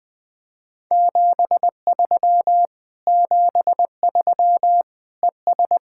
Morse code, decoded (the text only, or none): 73 73 ES